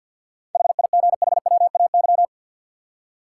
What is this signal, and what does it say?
Morse code, heard as a keyed tone, about 51 wpm, 700 Hz. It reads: HIGHWAY
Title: Seven Letter Words - 50wpm